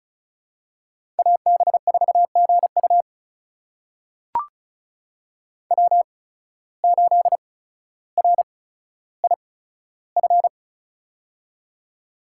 Morse code, heard as a keyed tone, about 36 wpm, spaced as for 15 wpm, 700 Hz. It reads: AB4GU W8RIF
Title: USA Call Signs - 5 Characters - 8x Character Spacing - 35wpm